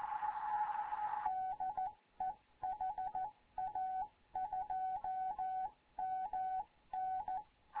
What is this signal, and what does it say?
Morse code, heard as a keyed tone, about 14 wpm, 750 Hz, with radio noise. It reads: DE HA2MN
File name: Morze-HA2MN / hangp / 2 11 GYAKORLAT